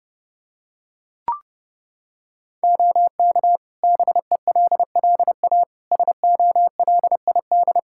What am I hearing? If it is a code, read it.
Morse code, heard as a keyed tone, about 32 wpm, 700 Hz. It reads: OK BELLA SOLID
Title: K1USN SST QSO Practice 30wpm